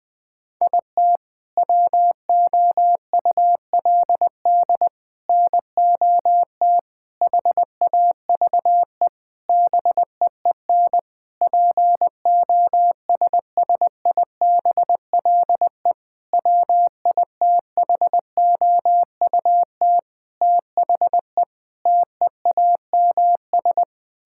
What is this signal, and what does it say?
Morse code, heard as a keyed tone, about 21 wpm, 700 Hz. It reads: IT WOULD NOT HAVE BEEN POSSIBLE WITHOUT THE TEAMS
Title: Sentences with 7-Word Phrases - 20wpm